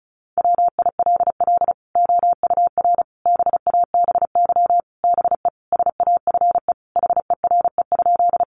Morse code, heard as a keyed tone, about 34 wpm, 700 Hz. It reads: WILL OUR BABY BE SAFE HERE?
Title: Sentences from Top 1300 Words 35wpm